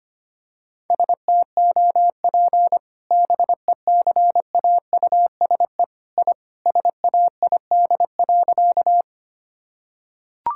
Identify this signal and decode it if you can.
Morse code, heard as a keyed tone, about 25 wpm, 700 Hz. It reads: STOP BECAUSE I SAID.
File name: practice / 25wpm / Sentences with 2-Word Phrases - 25wpm